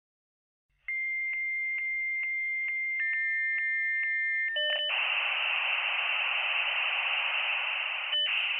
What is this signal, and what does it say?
Dial-up modem handshake.